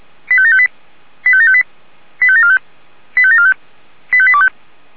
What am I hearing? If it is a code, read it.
Five-tone selective call (ZVEI), heard as quick runs of five tones, tones of about 70 ms. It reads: 85758 85758 85745 85745 85825
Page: /464/SDR/ZVEI/